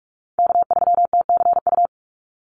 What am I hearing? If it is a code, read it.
Morse code, heard as a keyed tone, about 44 wpm, 700 Hz. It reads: K3TCU
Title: NA SOTA Chaser Callsigns 45wpm